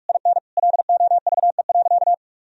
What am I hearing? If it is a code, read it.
Morse code, heard as a keyed tone, about 41 wpm, 700 Hz. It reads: IN LOVE.
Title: Sentences from Top 1000 Words - Review Part 3 - 45wpm